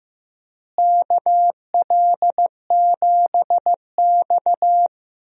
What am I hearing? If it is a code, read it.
Morse code, heard as a keyed tone, about 15 wpm, 700 Hz. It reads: KL7X
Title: DXCC-100 Call Signs - 4 Characters - Repeated 2x - 15wpm